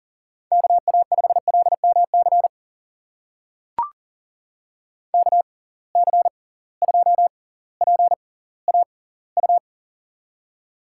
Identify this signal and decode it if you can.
Morse code, heard as a keyed tone, about 41 wpm, spaced as for 21 wpm, 700 Hz. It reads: KA5LMC KC2PAU
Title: USA Call Signs - 6 Characters - 6x Character Spacing - 40wpm